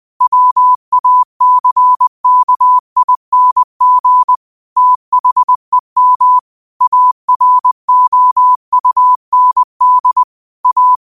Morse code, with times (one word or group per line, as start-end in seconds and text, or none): 0.2-4.4 WACKING
4.8-6.4 THEM
6.8-10.2 AROUND
10.7-11.0 A